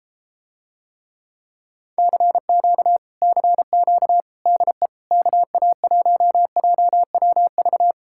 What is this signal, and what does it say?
Morse code, heard as a keyed tone, about 33 wpm, 700 Hz. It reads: CQ CQ DE KA1JWV